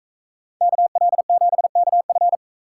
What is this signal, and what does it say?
Morse code, heard as a keyed tone, about 41 wpm, 700 Hz. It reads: KL7KF